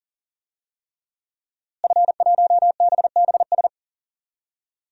Morse code, heard as a keyed tone, about 40 wpm, 700 Hz. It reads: F1BBS
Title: DXCC-100 Call Signs - 5 Characters - Repeated 3x - 40wpm